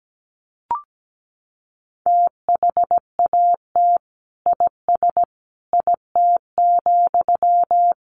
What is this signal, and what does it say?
Morse code, heard as a keyed tone, about 17 wpm, 700 Hz. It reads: THAT IS IT,
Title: Sentences from Top 800 Words 17wpm